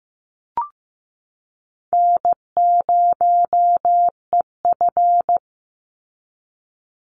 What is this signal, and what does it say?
Morse code, heard as a keyed tone, about 15 wpm, 700 Hz. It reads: N0EF
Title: Call Signs Repeated 2x - 15wpm